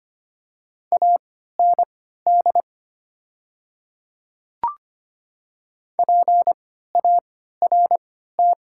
Morse code, heard as a keyed tone, about 25 wpm, spaced as for 14 wpm, 700 Hz. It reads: AND PART
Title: 3x Character Spacing - Top 300 Words - 25wpm